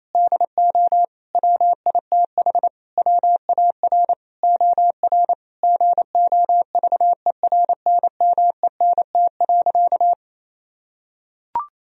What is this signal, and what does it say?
Morse code, heard as a keyed tone, about 28 wpm, 700 Hz. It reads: DO WITH WAR OR GOVERNMENT.
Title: Sentences from Top 1000 Words - Review Part 3 - 28wpm